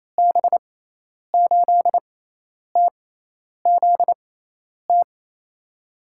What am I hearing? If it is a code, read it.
Morse code, heard as a keyed tone, about 29 wpm, spaced as for 10 wpm, 700 Hz. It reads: B8TZT
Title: USA Call Signs - 6 Characters - 6x Character Spacing - 28wpm